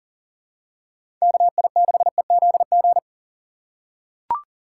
Morse code, heard as a keyed tone, about 40 wpm, 700 Hz. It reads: KI6EZG
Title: USA Call Signs - 2x Character Spacing - 40wpm